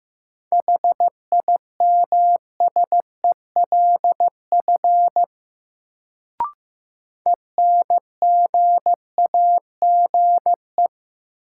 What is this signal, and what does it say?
Morse code, heard as a keyed tone, about 15 wpm, 700 Hz. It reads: HIMSELF ENGAGE
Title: Sentences from Top 1500 Words - New Word Drill - 15wpm